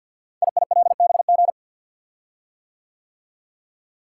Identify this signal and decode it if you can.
Morse code, heard as a keyed tone, about 50 wpm, 700 Hz. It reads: IILBG
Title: Code Groups - 5 Letters - 50wpm